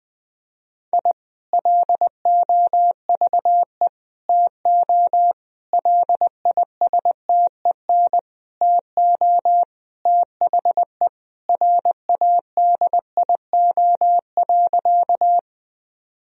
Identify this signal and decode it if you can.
Morse code, heard as a keyed tone, about 21 wpm, 700 Hz. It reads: I LOVE TO LISTEN TO THE RADIO.
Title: Sentences from Top 1000 Words - Review Part 3 - 20wpm